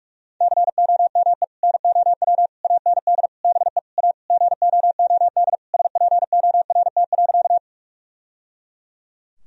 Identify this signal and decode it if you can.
Morse code, heard as a keyed tone, about 46 wpm, 700 Hz. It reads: COME NOW AND BE A GOOD SPORT.
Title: Sentences from Top 1000 Words - Review Part 4 - 45wpm